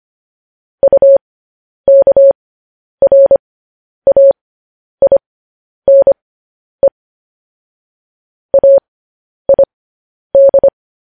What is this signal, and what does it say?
Morse code, heard as a keyed tone, about 25 wpm, spaced as for 10 wpm, 550 Hz. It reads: UKRAINE AID
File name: podcast / News headlines-1x-25x10